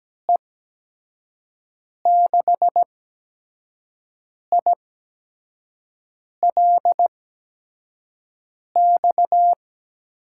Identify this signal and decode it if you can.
Morse code, heard as a keyed tone, about 17 wpm, spaced as for 5 wpm, 700 Hz. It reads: E6ILX
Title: USA Call Signs - 6 Characters - 8x Character Spacing - 17wpm